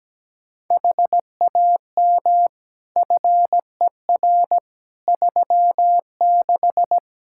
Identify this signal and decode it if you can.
Morse code, heard as a keyed tone, about 17 wpm, 700 Hz. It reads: HAM FER 36
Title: K1USN SST QSO Practice 17wpm